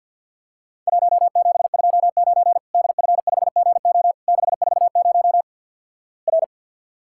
Morse code, heard as a keyed tone, about 50 wpm, 700 Hz. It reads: 1729 DF5GO 640 R
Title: Worked All Europe DX Contest - QTC Exchange with Repeats - 50wpm